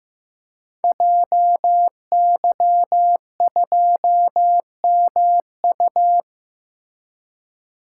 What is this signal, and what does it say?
Morse code, heard as a keyed tone, about 15 wpm, 700 Hz. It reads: JY2MU
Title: Code Groups - 5 Letter:Numbers - 15wpm